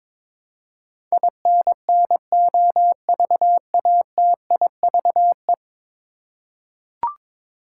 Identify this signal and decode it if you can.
Morse code, heard as a keyed tone, about 22 wpm, 700 Hz. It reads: INNOVATIVE
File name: practice / 22wpm / 10 Letter Words - First 3 Letters Given - 22wpm